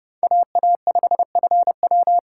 Morse code, heard as a keyed tone, about 38 wpm, 700 Hz. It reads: AA5FW